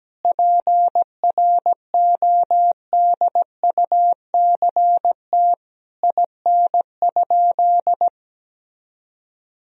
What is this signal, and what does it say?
Morse code, heard as a keyed tone, about 17 wpm, 700 Hz. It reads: PRODUCT IN?
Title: Sentences from Top 1000 Words - Review Part 2 - 17wpm